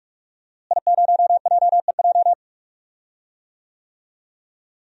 Morse code, heard as a keyed tone, about 46 wpm, 700 Hz. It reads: I0JEJ
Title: DXCC-100 Call Signs - 5 Characters - Repeated 2x - 45wpm